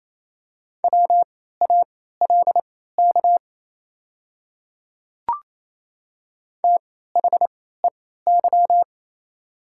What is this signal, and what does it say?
Morse code, heard as a keyed tone, about 28 wpm, spaced as for 16 wpm, 700 Hz. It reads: WALK THEY
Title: 3x Character Spacing - Top 300 Words - 28wpm